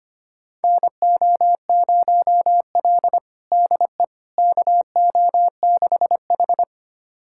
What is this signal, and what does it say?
Morse code, heard as a keyed tone, about 25 wpm, 700 Hz. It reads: NO0L DE KO6H
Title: USA Callsign DE Callsign - 4 Characters - 25wpm